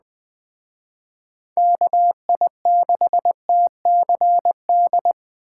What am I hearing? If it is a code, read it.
Morse code, heard as a keyed tone, about 20 wpm, 700 Hz. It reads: KI6TCD
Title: Speed Racing - USA Call Signs 17wpm